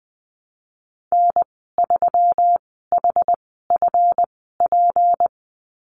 Morse code, heard as a keyed tone, about 20 wpm, spaced as for 14 wpm, 700 Hz. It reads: N3HFP
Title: 154 Lesson 25g - Letter p - Call Signs 20wpm